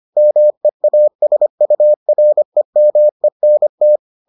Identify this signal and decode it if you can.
Morse code, heard as a keyed tone, about 25 wpm, 600 Hz. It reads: MEASUREMENT